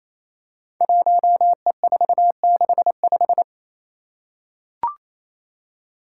Morse code, heard as a keyed tone, about 27 wpm, 700 Hz. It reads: 1E465
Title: Code Groups - 5 Letter:Numbers - 28wpm